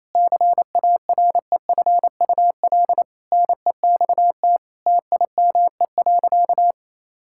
Morse code, heard as a keyed tone, about 28 wpm, 700 Hz. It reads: CAREFUL NEXT TIME.
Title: Sentences from Top 1600 Words 28wpm